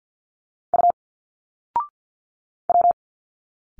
Morse code, heard as a keyed tone, about 49 wpm, 700 Hz. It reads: U W
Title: Single Letters 50wpm